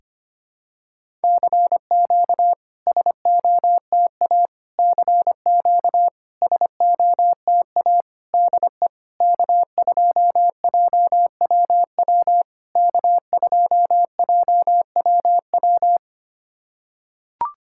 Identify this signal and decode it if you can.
Morse code, heard as a keyed tone, about 25 wpm, 700 Hz. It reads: CQ SOTA CQ SOTA DE K2JWW K2JWW
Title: SOTA QSO Practice 25wpm